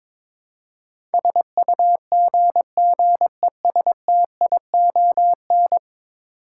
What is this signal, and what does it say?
Morse code, heard as a keyed tone, about 21 wpm, 700 Hz. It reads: SUGGESTION
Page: morse-code-ninja-practice-22wpm